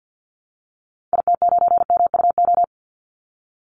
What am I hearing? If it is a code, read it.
Morse code, heard as a keyed tone, about 50 wpm, 700 Hz. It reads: IT9MUO
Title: Call Signs - 50wpm